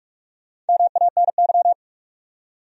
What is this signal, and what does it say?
Morse code, heard as a keyed tone, about 43 wpm, 700 Hz. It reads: MANY